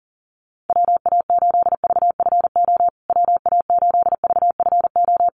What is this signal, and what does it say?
Morse code, heard as a keyed tone, about 40 wpm, 700 Hz. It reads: WA8VFO WA8VFO
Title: USA Call Signs Repeated 3x 40wpm